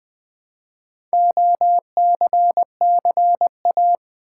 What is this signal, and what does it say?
Morse code, heard as a keyed tone, about 20 wpm, 700 Hz. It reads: OCCA
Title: Build A Word - 2 Letters At A Time - 8-Letter Words - 20wpm